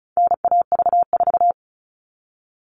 Morse code, heard as a keyed tone, about 34 wpm, 700 Hz. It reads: NAV4
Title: All Cut Numbers - 4-Digit 35wpm